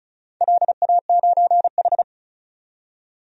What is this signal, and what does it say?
Morse code, heard as a keyed tone, about 36 wpm, 700 Hz. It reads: LA9H
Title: DXCC-100 Call Signs - 4 Characters - 35wpm